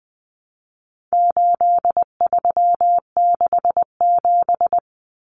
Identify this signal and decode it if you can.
Morse code, heard as a keyed tone, about 20 wpm, 700 Hz. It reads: 8367
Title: Common Cut Numbers - 4-Digit 20wpm